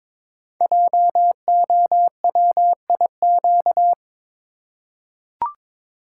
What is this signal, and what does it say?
Morse code, heard as a keyed tone, about 23 wpm, 700 Hz. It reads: JOWIQ
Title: Code Groups - 5 Letters - 22wpm